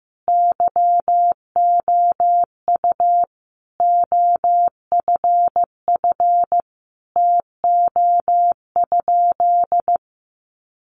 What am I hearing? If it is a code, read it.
Morse code, heard as a keyed tone, about 15 wpm, 700 Hz. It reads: YOU OFF TO?
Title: Sentences from Top 200 Words 15wpm